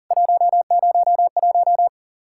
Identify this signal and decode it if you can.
Morse code, heard as a keyed tone, about 40 wpm, 700 Hz. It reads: 101
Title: Worked All Europe DX Contest - QTC Exchange - 40wpm